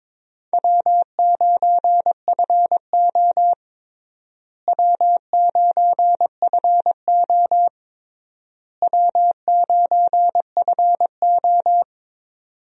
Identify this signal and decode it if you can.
Morse code, heard as a keyed tone, about 22 wpm, 700 Hz. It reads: W9FO W9FO W9FO
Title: USA Call Signs - 4 Characters - Repeated 3x - 22wpm